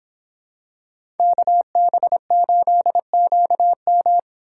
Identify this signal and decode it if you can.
Morse code, heard as a keyed tone, about 25 wpm, 700 Hz. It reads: KB8QM